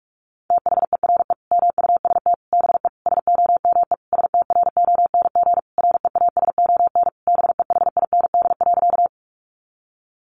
Morse code, heard as a keyed tone, about 44 wpm, 700 Hz. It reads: THERE MUST BE SOME STRONG REASON BEHIND.
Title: Sentences from Top 700 Words 45wpm